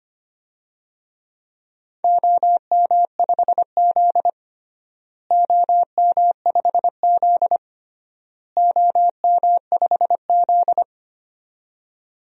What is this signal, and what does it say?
Morse code, heard as a keyed tone, about 25 wpm, 700 Hz. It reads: OM5Z OM5Z OM5Z
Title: DXCC-100 Call Signs - 4 Characters - Repeated 3x - 25wpm